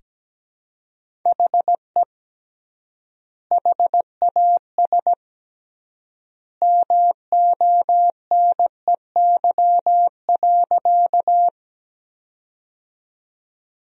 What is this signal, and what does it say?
Morse code, heard as a keyed tone, about 17 wpm, 700 Hz. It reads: HE HAS MONEY.
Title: 3x Word-Spacing - Sentences from Top 400 Words - 17wpm